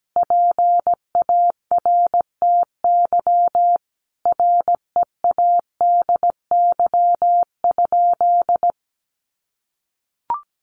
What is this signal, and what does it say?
Morse code, heard as a keyed tone, about 17 wpm, 700 Hz. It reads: PARTY READY?
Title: Sentences from Top 1400 Words 17wpm